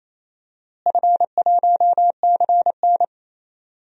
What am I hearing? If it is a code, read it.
Morse code, heard as a keyed tone, about 28 wpm, 700 Hz. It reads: F1CN
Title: DXCC-100 Call Signs - 4 Characters - Repeated 3x - 28wpm